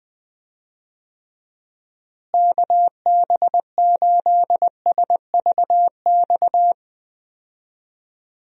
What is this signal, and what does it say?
Morse code, heard as a keyed tone, about 20 wpm, 700 Hz. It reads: KB8SVX
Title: USA Call Signs - 6 Characters - Repeated 2x - 20wpm